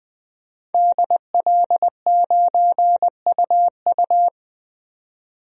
Morse code, heard as a keyed tone, about 20 wpm, 700 Hz. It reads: DL9UU
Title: DXCC-100 Call Signs - 5 Characters - 20wpm